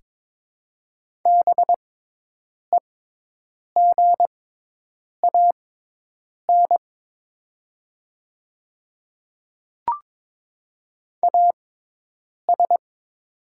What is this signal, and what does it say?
Morse code, heard as a keyed tone, about 22 wpm, spaced as for 8 wpm, 690 Hz. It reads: BEGAN AS
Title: 6x Character Spacing - Top 300 Words - 22wpm